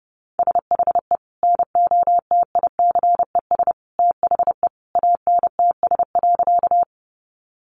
Morse code, heard as a keyed tone, about 30 wpm, 700 Hz. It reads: SHE NOTICES THE ANTS.
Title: Additional Easy Phrases 30wpm